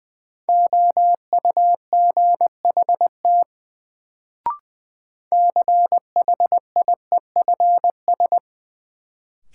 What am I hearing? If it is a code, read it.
Morse code, heard as a keyed tone, about 20 wpm, 700 Hz. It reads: OUGHT CHIEFS
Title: Sets of 5 Words Encore - New Word Drill - 20wpm